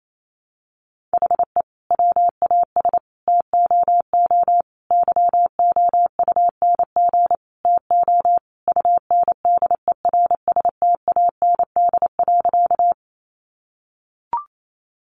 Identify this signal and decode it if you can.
Morse code, heard as a keyed tone, about 28 wpm, 700 Hz. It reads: HE WAS TOO YOUNG TO UNDERSTAND.
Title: Sentences from Top 600 Words 28wpm